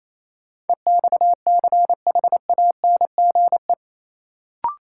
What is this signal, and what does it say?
Morse code, heard as a keyed tone, about 28 wpm, 700 Hz. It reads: EXCHANGE
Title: Sets of 5 Words - New Word Drill - 28wpm